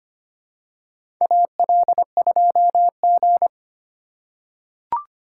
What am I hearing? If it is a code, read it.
Morse code, heard as a keyed tone, about 25 wpm, 700 Hz. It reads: AL2G
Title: USA Call Signs - 4 Characters - 4x Character Spacing - 25wpm